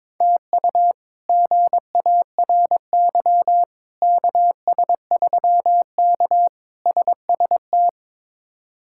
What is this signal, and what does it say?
Morse code, heard as a keyed tone, about 22 wpm, 700 Hz. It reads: TU GARY KS3K SST
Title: K1USN SST QSO Practice - 22wpm